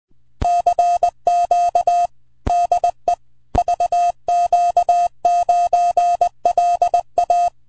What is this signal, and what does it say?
Morse code, heard as a keyed tone, about 20 wpm, 710 Hz. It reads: CQ DE VQ9LA
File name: vq9la / CX5BW 80MTR CW 01SEP2008 0114Z